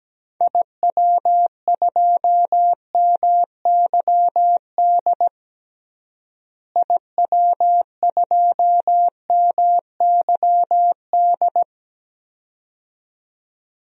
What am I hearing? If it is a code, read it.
Morse code, heard as a keyed tone, about 17 wpm, 700 Hz. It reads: IW2MYD IW2MYD
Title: DXCC-100 Call Signs - 6 Characters - Repeated 2x - 17wpm